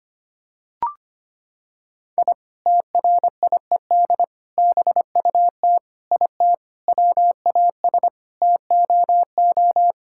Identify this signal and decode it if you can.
Morse code, heard as a keyed tone, about 25 wpm, 700 Hz. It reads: I TRIED BUT IT WAS TOO
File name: practice / 25wpm / Sentences with 3-Word Phrases - 25wpm